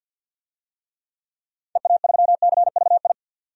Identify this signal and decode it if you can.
Morse code, heard as a keyed tone, about 52 wpm, 700 Hz. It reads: EA3CVI
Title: DXCC-100 Call Signs - 6 Characters - 50wpm